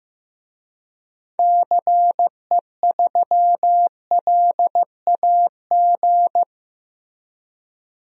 Morse code, heard as a keyed tone, about 15 wpm, 700 Hz. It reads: CE3LAG
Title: DXCC-100 Call Signs - 6 Characters - 15wpm